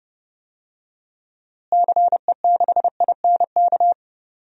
Morse code, heard as a keyed tone, about 31 wpm, 700 Hz. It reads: CE6INK
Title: DXCC-100 Call Signs - 6 Characters - Repeated 2x - 30wpm